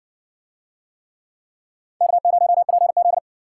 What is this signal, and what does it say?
Morse code, heard as a keyed tone, about 60 wpm, 700 Hz. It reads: K0PZ